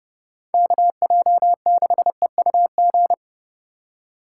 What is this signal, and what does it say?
Morse code, heard as a keyed tone, about 30 wpm, 700 Hz. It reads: KJ6EUG